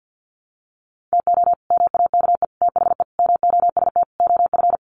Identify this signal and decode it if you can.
Morse code, heard as a keyed tone, about 50 wpm, 700 Hz. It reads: TO MAKE THE MOST OF
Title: Sets of 5 Words Encore2 50wpm